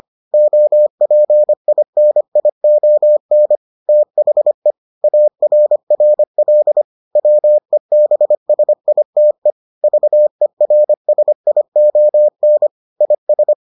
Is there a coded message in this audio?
Yes: Morse code from 0.3 to 13.5 s, heard as a keyed tone, about 25 wpm, 600 Hz. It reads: OPINION THE ARRL WEBSITE VERSION IS